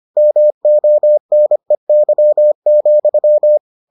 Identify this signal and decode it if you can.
Morse code, heard as a keyed tone, about 25 wpm, 600 Hz. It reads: MONEY,